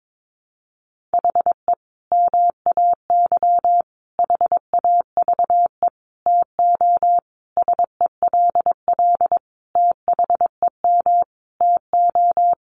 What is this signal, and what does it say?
Morse code, heard as a keyed tone, about 22 wpm, 700 Hz. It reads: HE MAY HAVE TO SELL THEM TO
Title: Sentences from Top 1100 Words 22wpm